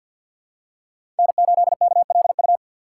Morse code, heard as a keyed tone, about 52 wpm, 700 Hz. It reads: N8KLU
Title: USA Call Signs - 5 Characters - 4x Character Spacing - 50wpm